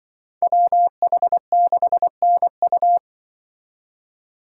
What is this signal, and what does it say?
Morse code, heard as a keyed tone, about 23 wpm, 700 Hz. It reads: WH6NU
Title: Speed Racing - USA 2x2 Call Signs 20wpm